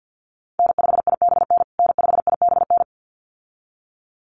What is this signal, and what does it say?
Morse code, heard as a keyed tone, about 50 wpm, 700 Hz. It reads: N5IBN N5IBN
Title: USA Call Signs Repeated 2x 50wpm